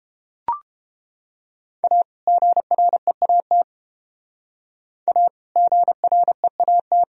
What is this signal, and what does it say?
Morse code, heard as a keyed tone, about 32 wpm, 700 Hz. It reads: A GREAT A GREAT